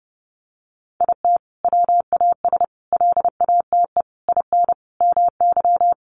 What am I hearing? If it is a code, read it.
Morse code, heard as a keyed tone, about 28 wpm, 700 Hz. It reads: IT WAS LATE IN MY
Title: Sentences from Top 1300 Words 30wpm